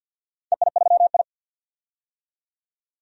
Morse code, heard as a keyed tone, about 46 wpm, 700 Hz. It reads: EI3I
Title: DXCC-100 Call Signs - 4 Characters - 50wpm